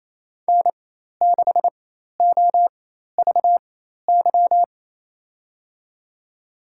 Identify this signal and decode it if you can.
Morse code, heard as a keyed tone, about 28 wpm, spaced as for 13 wpm, 700 Hz. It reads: N6OVY
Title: USA Call Signs - 5 Characters - 4x Character Spacing - 28wpm